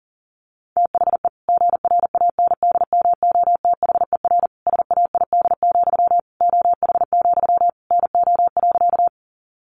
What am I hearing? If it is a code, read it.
Morse code, heard as a keyed tone, about 40 wpm, 700 Hz. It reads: THE GRANDMOTHER SAID, OH, NO.
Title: Sentences from Top 1200 Words 40wpm